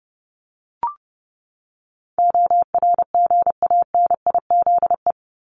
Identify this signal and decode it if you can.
Morse code, heard as a keyed tone, about 30 wpm, 700 Hz. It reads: ORGANIZE
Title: Top 1751-2000 Words 30wpm